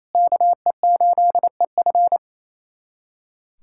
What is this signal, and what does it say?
Morse code, heard as a keyed tone, about 28 wpm, 700 Hz. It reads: KE8EF
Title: USA Callsign DE Callsign - 5 Characters - 28wpm